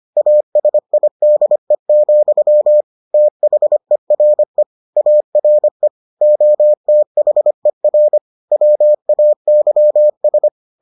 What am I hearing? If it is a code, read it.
Morse code, heard as a keyed tone, about 25 wpm, 600 Hz. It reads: ASIDE, THERE ARE OTHER WAYS